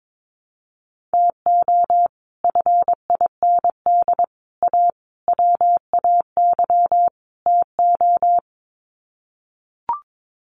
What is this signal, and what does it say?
Morse code, heard as a keyed tone, about 22 wpm, 700 Hz. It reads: TO FIND A WAY TO
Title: Sets of 5 Words 22wpm